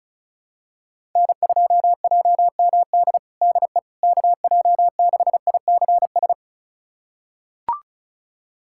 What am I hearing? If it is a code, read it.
Morse code, heard as a keyed tone, about 36 wpm, 700 Hz. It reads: N2JMD DE KJ6ICS
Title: USA Callsign DE Callsign 35wpm